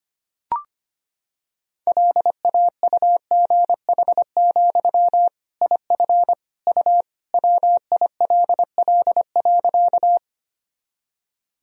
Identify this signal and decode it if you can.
Morse code, heard as a keyed tone, about 25 wpm, 700 Hz. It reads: LAUGH, IF U WILL.
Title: Sentences from Top 500 Words and Abbreviations 25wpm